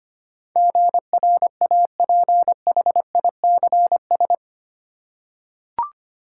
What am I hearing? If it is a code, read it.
Morse code, heard as a keyed tone, about 25 wpm, 700 Hz. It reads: GRAPHICS
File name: 8 Letter Words - First 3 Letters Given - 25wpm